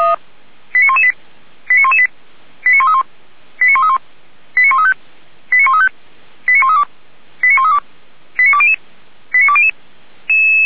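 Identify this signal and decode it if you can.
Five-tone selective call (ZVEI), heard as quick runs of five tones, tones of about 70 ms. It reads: 89118 89118 89131 89131 89135 89135 89132 89132 89200 89200
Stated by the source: Five-tone ZVEI selective call 89103 received on an SDR.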